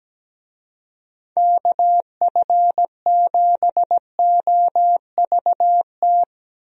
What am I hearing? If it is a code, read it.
Morse code, heard as a keyed tone, about 17 wpm, 700 Hz. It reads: KF7OVT
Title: USA Call Signs - 6 Characters - Repeated 2x - 17wpm